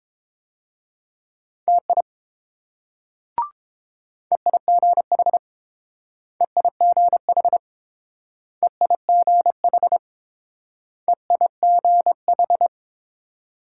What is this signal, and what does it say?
Morse code, heard as a keyed tone, about 28 wpm, 700 Hz. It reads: TI EIGH EIGH EIGH EIGH